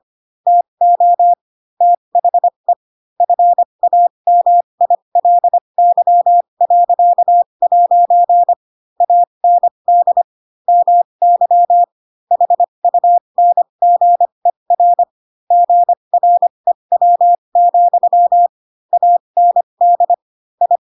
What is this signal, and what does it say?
Morse code, heard as a keyed tone, about 25 wpm, spaced as for 22 wpm, 700 Hz. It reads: TO THE FAMILY.' AND MY HUNGER GREW, AND I
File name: Pinocchio / Pinocchio ch7 18 25 ik1hge